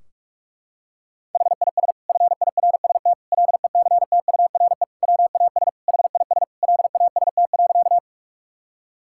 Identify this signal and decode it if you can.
Morse code, heard as a keyed tone, about 46 wpm, 700 Hz. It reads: HIS FIRST LECTURE WAS HIS LAST.
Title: Sentences from Top 1600 Words 45wpm